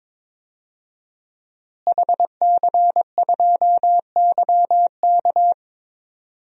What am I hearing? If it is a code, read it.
Morse code, heard as a keyed tone, about 22 wpm, 700 Hz. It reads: HC2YK
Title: DXCC-100 Call Signs - 5 Characters - Repeated 2x - 22wpm